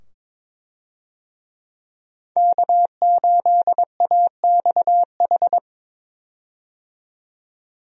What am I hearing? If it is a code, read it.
Morse code, heard as a keyed tone, about 21 wpm, 700 Hz. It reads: K8AXH
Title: USA Call Signs - 5 Characters - 22wpm